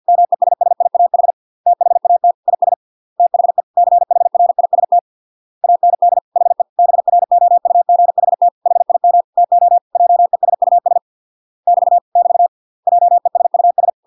Morse code, heard as a keyed tone, about 50 wpm, 700 Hz. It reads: MESSIAH THAT IS THE CHRIST AND HE BROUGHT HIM TO JESUS <BT> <BT> JESUS